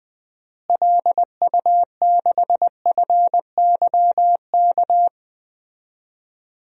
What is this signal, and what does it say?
Morse code, heard as a keyed tone, about 20 wpm, 700 Hz. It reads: LU6FYK
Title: DXCC-100 Call Signs - 6 Characters - Repeated 3x - 20wpm